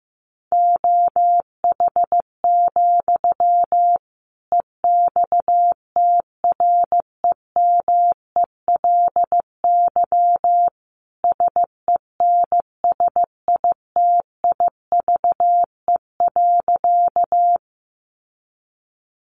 Morse code, heard as a keyed tone, about 15 wpm, 700 Hz. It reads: OH, EXTREMELY SENSITIVE.
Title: Sentences from Top 1400 Words 15wpm